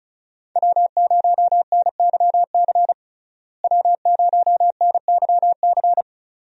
Morse code, heard as a keyed tone, about 34 wpm, 700 Hz. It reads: W0NYC W0NYC
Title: USA Call Signs - 5 Characters - Repeated 2x - 35wpm